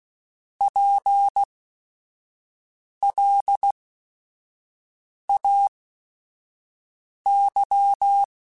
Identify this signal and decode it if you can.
Morse code, heard as a keyed tone, about 16 wpm, spaced as for 5 wpm, 800 Hz. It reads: PLAY
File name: General / General-mp3s / Houses @5